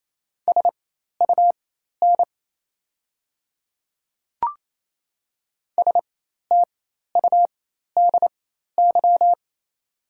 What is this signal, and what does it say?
Morse code, heard as a keyed tone, about 28 wpm, spaced as for 13 wpm, 690 Hz. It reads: SUN STUDY
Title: 4x Character Spacing - Top 300 Words - 28wpm